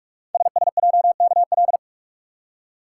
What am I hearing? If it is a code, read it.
Morse code, heard as a keyed tone, about 44 wpm, 700 Hz. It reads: SSJKL
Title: Code Groups - 5 Letters - 45wpm